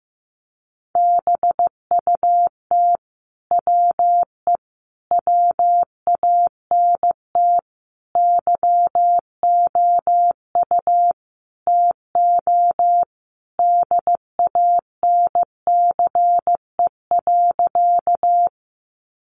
Morse code, heard as a keyed tone, about 15 wpm, 700 Hz. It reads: BUT WE WANT YOU TO DANCE.
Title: Sentences from Top 700 Words 15wpm